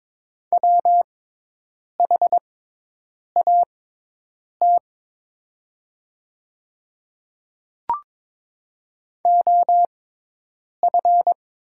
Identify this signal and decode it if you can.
Morse code, heard as a keyed tone, about 22 wpm, spaced as for 8 wpm, 700 Hz. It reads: WHAT OF
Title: 6x Character Spacing - Top 300 Words - 22wpm